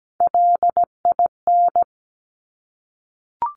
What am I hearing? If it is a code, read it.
Morse code, heard as a keyed tone, about 17 wpm, 690 Hz. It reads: LIN
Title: Top Three Letter Combinations in Words 17wpm